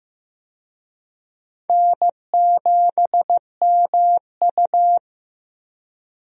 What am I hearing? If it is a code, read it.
Morse code, heard as a keyed tone, about 15 wpm, 700 Hz. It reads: N7MU
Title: USA Call Signs - 4 Characters - 15wpm